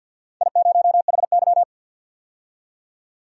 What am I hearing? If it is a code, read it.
Morse code, heard as a keyed tone, about 47 wpm, 700 Hz. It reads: I0HY
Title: DXCC-100 Call Signs - 4 Characters - 50wpm